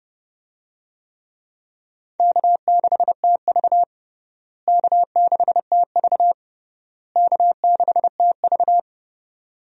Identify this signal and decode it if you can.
Morse code, heard as a keyed tone, about 30 wpm, 700 Hz. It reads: K6TV K6TV K6TV
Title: USA Call Signs - 4 Characters - Repeated 3x - 30wpm